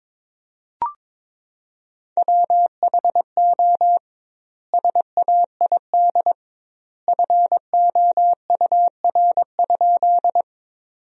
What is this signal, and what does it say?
Morse code, heard as a keyed tone, about 22 wpm, 700 Hz. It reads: WHO SAID FOUR?
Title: 2x Word-Spacing - Sentences from Top 300 Words - 22wpm